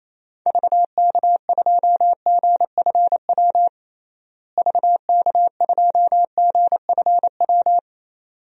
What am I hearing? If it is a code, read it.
Morse code, heard as a keyed tone, about 28 wpm, 700 Hz. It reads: VK2GFW VK2GFW
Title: DXCC-100 Call Signs - 6 Characters - Repeated 2x - 28wpm